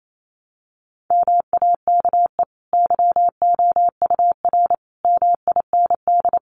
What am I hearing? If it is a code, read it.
Morse code, heard as a keyed tone, about 28 wpm, 700 Hz. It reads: MAKE YOUR MIND
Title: Sentences from Top 700 Words 28wpm